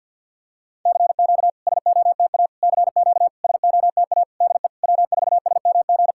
Morse code, heard as a keyed tone, about 48 wpm, 700 Hz. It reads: CQ SOTA CQ SOTA DE W4SMG
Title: SOTA QSO Practice 50wpm